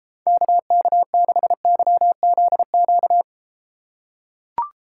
Morse code, heard as a keyed tone, about 31 wpm, 700 Hz. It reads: KK6YZQ